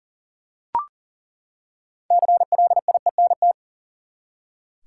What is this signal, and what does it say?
Morse code, heard as a keyed tone, about 40 wpm, 700 Hz. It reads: CLIENT
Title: Six Letter Words - 40wpm